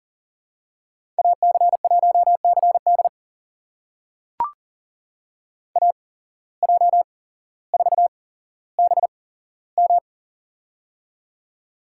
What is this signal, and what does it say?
Morse code, heard as a keyed tone, about 41 wpm, spaced as for 17 wpm, 700 Hz. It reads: AC1CD AJ4BM
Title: USA Call Signs - 8x Character Spacing - 40wpm